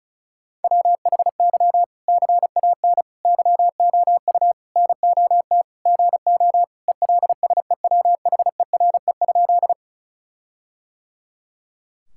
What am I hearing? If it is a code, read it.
Morse code, heard as a keyed tone, about 36 wpm, 700 Hz. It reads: WHY CAN YOU NOT GO ELSEWHERE?
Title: Sentences from Top 1600 Words 35wpm